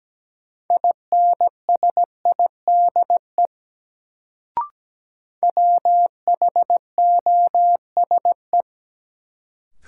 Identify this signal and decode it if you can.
Morse code, heard as a keyed tone, about 17 wpm, 700 Hz. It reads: INSIDE WHOSE
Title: Sentences from Top 1200 Words - New Word Drill Alt - 17wpm